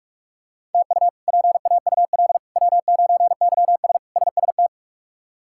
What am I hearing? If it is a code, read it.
Morse code, heard as a keyed tone, about 46 wpm, 700 Hz. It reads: TU PAUL W9YS SST
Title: K1USN SST QSO Practice - 45wpm